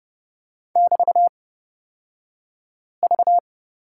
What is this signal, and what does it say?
Morse code, heard as a keyed tone, about 31 wpm, spaced as for 5 wpm, 700 Hz. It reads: <BT>V
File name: Lesson 36d - BT - Two Characters 30wpm